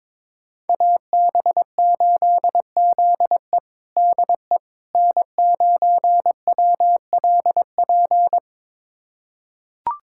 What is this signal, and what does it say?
Morse code, heard as a keyed tone, about 22 wpm, 700 Hz. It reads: AB8ZE DE N9WLP
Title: USA Callsign DE Callsign - 5 Characters - 22wpm